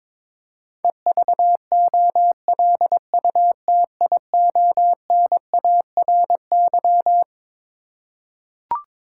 Morse code, heard as a keyed tone, about 22 wpm, 700 Hz. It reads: EVOLUTIONARY
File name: Twelve Letter Words - 22wpm